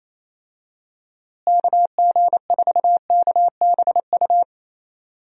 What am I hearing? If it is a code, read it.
Morse code, heard as a keyed tone, about 28 wpm, 700 Hz. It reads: KG4KBU
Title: USA Call Signs - 6 Characters - Repeated 2x - 28wpm